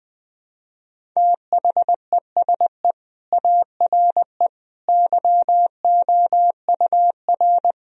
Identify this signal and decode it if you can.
Morse code, heard as a keyed tone, about 20 wpm, 700 Hz. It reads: THESE ARE YOUR
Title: Sentences from Top 1000 Words - Review Part 1 - 20wpm